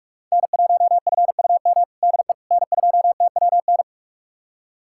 Morse code, heard as a keyed tone, about 47 wpm, 700 Hz. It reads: N1FUM DE N2TWN
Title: USA Callsign DE Callsign - 5 Characters - 45wpm